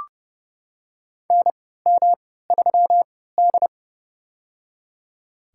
Morse code, heard as a keyed tone, about 30 wpm, spaced as for 17 wpm, 700 Hz. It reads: NM3D
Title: USA Call Signs - 4 Characters - 3x Character Spacing - 30wpm